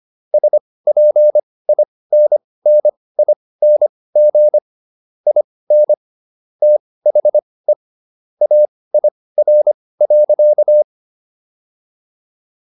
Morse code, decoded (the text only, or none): SPINNING IN THE AIR.